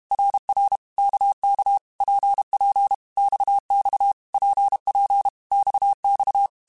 Morse code, heard as a keyed tone, about 32 wpm, 800 Hz. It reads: RR KK PP XX PP XX
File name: Basic Letter Groups By7Groups / RKPX / RKPX@30